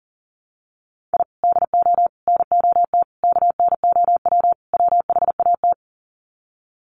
Morse code, heard as a keyed tone, about 40 wpm, 700 Hz. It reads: I DO NOT KNOW WHAT
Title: Sets of 5 Words 40wpm